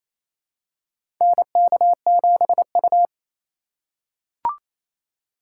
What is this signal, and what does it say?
Morse code, heard as a keyed tone, about 28 wpm, 700 Hz. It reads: NK7U
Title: K1USN SST QSO Practice - 28wpm